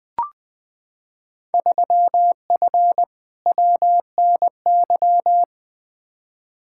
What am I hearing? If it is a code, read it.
Morse code, heard as a keyed tone, about 21 wpm, 700 Hz. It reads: 3F WNY
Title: Field Day Exchanges - 20wpm